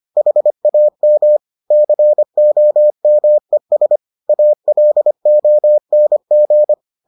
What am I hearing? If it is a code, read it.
Morse code, heard as a keyed tone, about 25 wpm, 600 Hz. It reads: HAM COMES ALONG